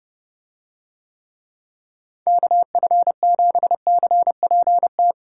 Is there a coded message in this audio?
Yes: Morse code from 2.3 to 5.1 s, heard as a keyed tone, about 31 wpm, 700 Hz. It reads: KF7CPT